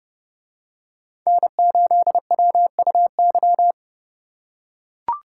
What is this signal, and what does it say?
Morse code, heard as a keyed tone, about 29 wpm, 700 Hz. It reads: N8WUY